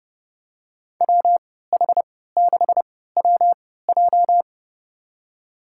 Morse code, heard as a keyed tone, about 30 wpm, spaced as for 17 wpm, 700 Hz. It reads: WH6WJ
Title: USA Call Signs - 5 Characters - 3x Character Spacing - 30wpm